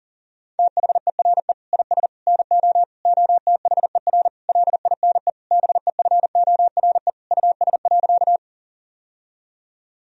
Morse code, heard as a keyed tone, about 40 wpm, 700 Hz. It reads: THERE IS NO OTHER LINE BEFORE US.
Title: Sentences from Top 1000 Words - Review Part 1 - 40wpm